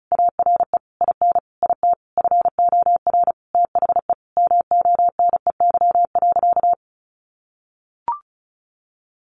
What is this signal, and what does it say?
Morse code, heard as a keyed tone, about 34 wpm, 700 Hz. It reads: ARE IN IT FOR THE MONEY.
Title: Sentences from Top 1300 Words 35wpm